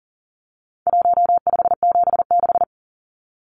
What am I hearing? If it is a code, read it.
Morse code, heard as a keyed tone, about 40 wpm, 700 Hz. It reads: 1576
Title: Numbers - Contest Style - 4-Digit 40wpm